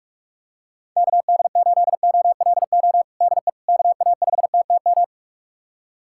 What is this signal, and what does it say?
Morse code, heard as a keyed tone, about 46 wpm, 700 Hz. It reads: KD8OLO DE KA5TTM
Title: USA Callsign DE Callsign 45wpm